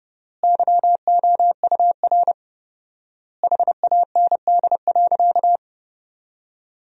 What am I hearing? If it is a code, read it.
Morse code, heard as a keyed tone, about 32 wpm, 700 Hz. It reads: YOUR HAND.
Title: 4x Word-Spacing - Sentences from Top 400 Words - 30wpm